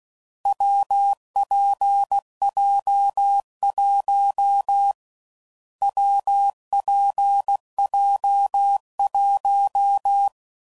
Morse code, heard as a keyed tone, about 16 wpm, 800 Hz. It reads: WPJ1 WPJ1
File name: Combinations / Difficult Combos / WPJ WPJ1